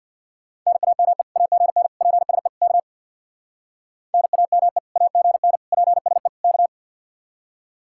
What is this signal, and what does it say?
Morse code, heard as a keyed tone, about 62 wpm, 700 Hz. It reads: NAME AGN PSE K NAME AGN PSE K
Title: Speed-Racing Common QSO Phrases Encore - 50wpm